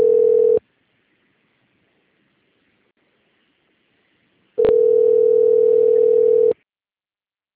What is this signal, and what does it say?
Ringback tone (North America), 440 + 480 Hz.